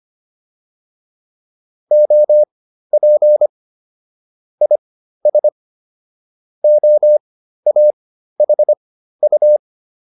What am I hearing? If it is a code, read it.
Morse code, heard as a keyed tone, about 25 wpm, spaced as for 13 wpm, 600 Hz. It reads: OP IS OAHU
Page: intermediate-practice-files